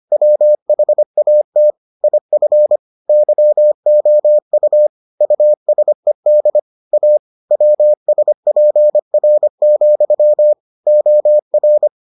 Morse code, heard as a keyed tone, about 25 wpm, 600 Hz. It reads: WHAT IF YOU USED A WSPR, OR